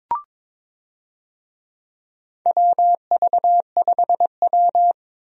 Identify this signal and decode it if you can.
Morse code, heard as a keyed tone, about 23 wpm, 700 Hz. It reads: WV5W
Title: USA Call Signs - 4 Characters - 22wpm